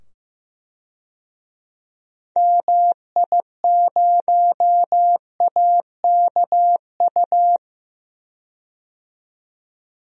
Morse code, heard as a keyed tone, about 15 wpm, 700 Hz. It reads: MI0AKU
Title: DXCC-100 Call Signs - 6 Characters - 15wpm